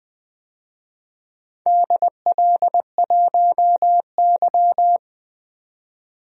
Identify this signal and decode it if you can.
Morse code, heard as a keyed tone, about 21 wpm, 700 Hz. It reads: DL1Y
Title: DXCC-100 Call Signs - 4 Characters - 20wpm